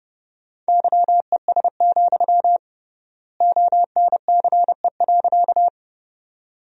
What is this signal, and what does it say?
Morse code, heard as a keyed tone, about 30 wpm, 700 Hz. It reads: YES, ONCE.
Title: 3x Word-Spacing - Sentences from Top 500 Words - 30wpm